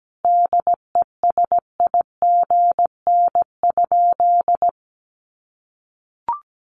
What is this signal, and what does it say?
Morse code, heard as a keyed tone, about 17 wpm, 700 Hz. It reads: DESIGN?
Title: Sentences from Top 800 Words 17wpm